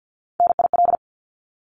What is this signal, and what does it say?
Morse code, heard as a keyed tone, about 50 wpm, 700 Hz. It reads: NIL
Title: Abbreviations Repeated 3x - 50wpm